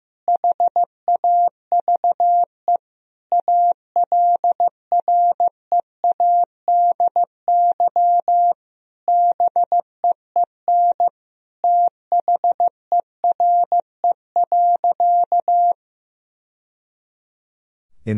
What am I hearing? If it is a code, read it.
Morse code, heard as a keyed tone, about 15 wpm, 700 Hz. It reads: HAVE ALREADY BEEN THERE.
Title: Sentences from Top 1000 Words - Review Part 3 - 15wpm